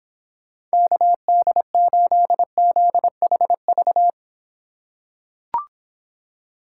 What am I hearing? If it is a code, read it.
Morse code, heard as a keyed tone, about 26 wpm, 700 Hz. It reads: KD8ZHV